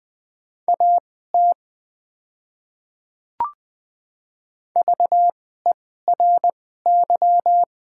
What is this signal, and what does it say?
Morse code, heard as a keyed tone, about 20 wpm, spaced as for 14 wpm, 700 Hz. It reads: AT VERY
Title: 2x Character Spacing - Top 200 Words - 20wpm